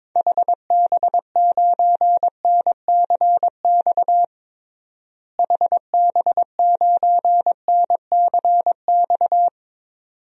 Morse code, heard as a keyed tone, about 22 wpm, 700 Hz. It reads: HB9NCX HB9NCX
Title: DXCC-100 Call Signs - 6 Characters - Repeated 2x - 22wpm